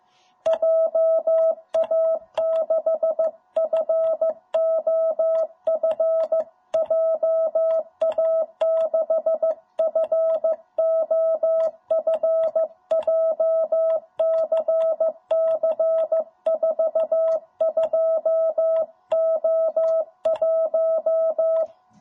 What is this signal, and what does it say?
Morse code, heard as a keyed tone, about 14 wpm, 660 Hz. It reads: JA6FOF JA6FOF JCC 42O1
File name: jh8cbh / 2021cw